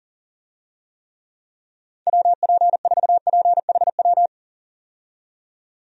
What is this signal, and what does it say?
Morse code, heard as a keyed tone, about 40 wpm, 700 Hz. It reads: WP4PHW
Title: USA Call Signs - 6 Characters - 40wpm